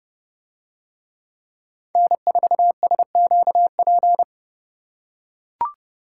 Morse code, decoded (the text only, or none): N4SQP